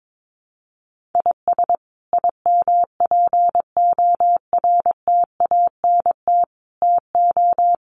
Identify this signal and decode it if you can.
Morse code, heard as a keyed tone, about 22 wpm, 700 Hz. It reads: IS IMPORTANT TO
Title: Sets of 5 Words 22wpm